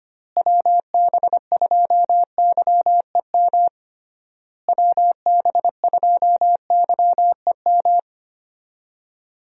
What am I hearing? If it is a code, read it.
Morse code, heard as a keyed tone, about 25 wpm, 700 Hz. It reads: WB2YEM WB2YEM
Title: USA Call Signs - 6 Characters - Repeated 3x - 25wpm